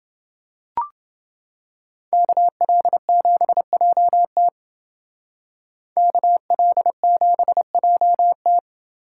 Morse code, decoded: KL7JT KL7JT

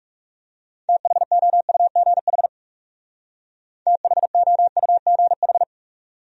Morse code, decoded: THOUGH THOUGH

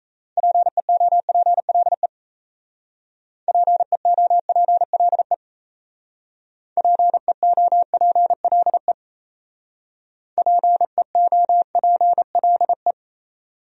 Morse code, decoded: PEOPLE PEOPLE PEOPLE PEOPLE

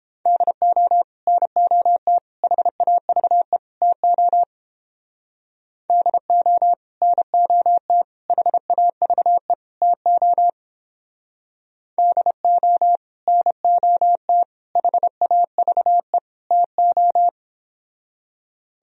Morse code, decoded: DO NOT HAVE TO DO NOT HAVE TO DO NOT HAVE TO

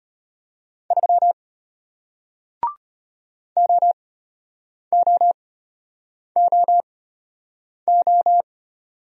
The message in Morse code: 3 OOOO

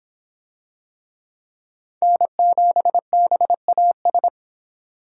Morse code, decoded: N7BAS

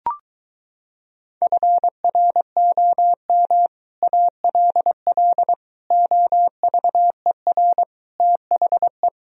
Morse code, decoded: FROM ALL OVER THE

